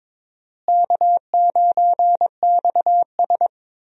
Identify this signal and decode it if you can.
Morse code, heard as a keyed tone, about 21 wpm, 700 Hz. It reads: K9XS
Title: USA Call Signs - 4 Characters - Repeated 3x - 22wpm